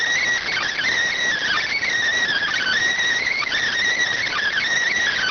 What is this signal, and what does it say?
Emergency Alert System (SAME) tones (header bursts), heard as harsh buzzing bursts. It is heard from 0.4 to 4.7 s.